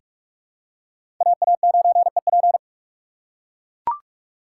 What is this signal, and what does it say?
Morse code, heard as a keyed tone, about 42 wpm, 700 Hz. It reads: AA9EP
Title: USA Call Signs - 5 Characters - 4x Character Spacing - 45wpm